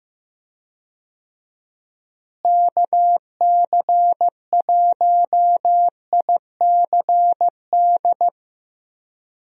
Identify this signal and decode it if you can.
Morse code, heard as a keyed tone, about 15 wpm, 700 Hz. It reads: KC1ICD
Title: USA Call Signs - 6 Characters - Repeated 2x - 15wpm